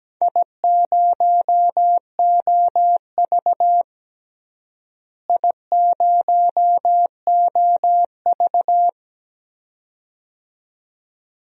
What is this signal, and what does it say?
Morse code, heard as a keyed tone, about 17 wpm, 700 Hz. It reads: I0OV I0OV
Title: DXCC-100 Call Signs - 4 Characters - Repeated 2x - 17wpm